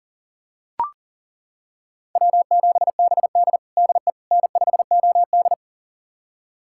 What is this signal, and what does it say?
Morse code, heard as a keyed tone, about 40 wpm, 700 Hz. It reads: W7BD DE N5OD
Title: USA Callsign DE Callsign - 4 Characters - 40wpm